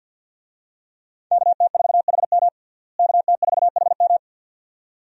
Morse code, heard as a keyed tone, about 52 wpm, 700 Hz. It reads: KT4HM KT4HM
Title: USA Call Signs - 5 Characters - Repeated 2x - 50wpm